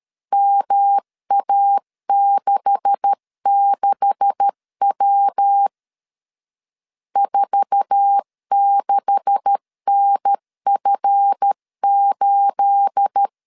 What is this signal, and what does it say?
Morse code, heard as a keyed tone, about 13 wpm, 800 Hz. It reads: MA66W 46NF8
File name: LEZIONI / 9 286 12wpm 2305